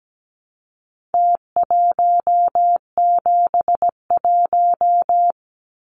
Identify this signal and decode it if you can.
Morse code, heard as a keyed tone, about 17 wpm, 700 Hz. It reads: T171
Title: Numbers - Contest Style - 4-Digit 17wpm